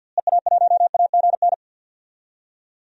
Morse code, heard as a keyed tone, about 52 wpm, 700 Hz. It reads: EA1AGN
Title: DXCC-100 Call Signs - 6 Characters - Repeated 3x - 50wpm